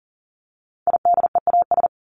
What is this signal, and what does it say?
Morse code, heard as a keyed tone, about 40 wpm, 700 Hz. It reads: IDEAS
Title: ES:S Suffix - Top 100 Words with Base Words 40wpm